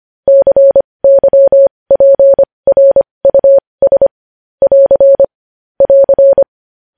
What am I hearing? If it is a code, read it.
Morse code, heard as a keyed tone, about 25 wpm, spaced as for 20 wpm, 550 Hz. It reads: CYPRUS <AR> <AR>
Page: News Headlines in Morse Code at 20 WPM